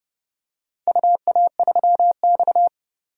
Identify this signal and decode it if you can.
Morse code, heard as a keyed tone, about 29 wpm, 700 Hz. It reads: UA3X
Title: DXCC-100 Call Signs - 4 Characters - 30wpm